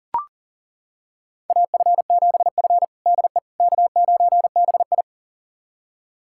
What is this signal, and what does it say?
Morse code, heard as a keyed tone, about 40 wpm, 700 Hz. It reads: AF7F DE K9BI
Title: USA Callsign DE Callsign - 4 Characters - 40wpm